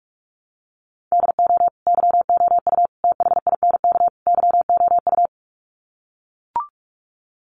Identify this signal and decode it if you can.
Morse code, heard as a keyed tone, about 44 wpm, 700 Hz. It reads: DO YOU THINK YOU
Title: Sets of 4 Words 45wpm